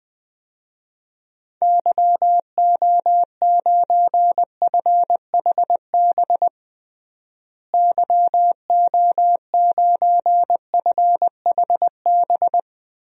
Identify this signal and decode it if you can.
Morse code, heard as a keyed tone, about 20 wpm, 700 Hz. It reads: YO9FHB YO9FHB
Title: DXCC-100 Call Signs - 6 Characters - Repeated 2x - 20wpm